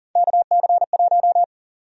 Morse code, heard as a keyed tone, about 40 wpm, 700 Hz. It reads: KC1
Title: USA Call Sign Prefixes - Repeated 2x - 40wpm